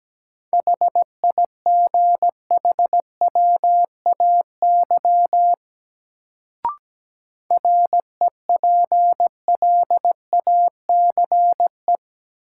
Morse code, heard as a keyed tone, about 17 wpm, 700 Hz. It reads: HIGHWAY REPLACE
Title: Sentences from Top 1400 Words - New Word Drill Alt - 17wpm